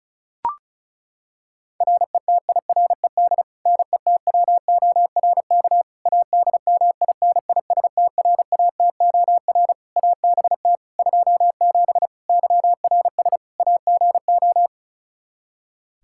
Morse code, heard as a keyed tone, about 36 wpm, 700 Hz. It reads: RETIRED NETWORK ADMINISTRATOR ABT 27 YRS AGO
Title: K1USN SST QSO Practice 35wpm